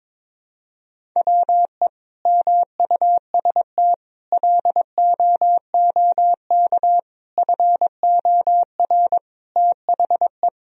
Morse code, heard as a keyed tone, about 22 wpm, 700 Hz. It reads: WE MUST LOOK FOR THE
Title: Sentences from Top 1000 Words - Review Part 2 - 22wpm